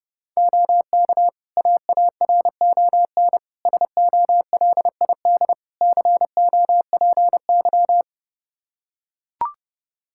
Morse code, heard as a keyed tone, about 30 wpm, 700 Hz. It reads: OK AARON SOLID COPY
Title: K1USN SST QSO Practice 30wpm